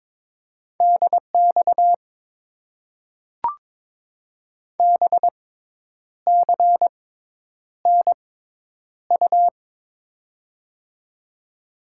Morse code, heard as a keyed tone, about 23 wpm, spaced as for 9 wpm, 700 Hz. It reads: DX BCNU
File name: Top 90 QSO Elements - 6x Character Spacing - 22wpm